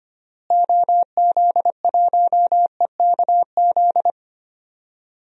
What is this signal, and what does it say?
Morse code, heard as a keyed tone, about 25 wpm, 700 Hz. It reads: OZ1EKZ
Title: DXCC-100 Call Signs - 6 Characters - Repeated 3x - 25wpm